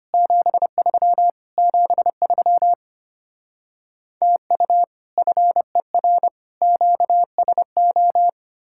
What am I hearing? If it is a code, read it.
Morse code, heard as a keyed tone, about 33 wpm, 700 Hz. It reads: 73 73 TU FER QSO